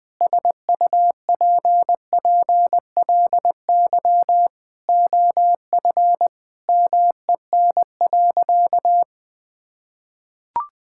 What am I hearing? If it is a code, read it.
Morse code, heard as a keyed tone, about 21 wpm, 700 Hz. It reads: SUPPLY OF MEN.
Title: Sentences from Top 1000 Words - Review Part 5 - 20wpm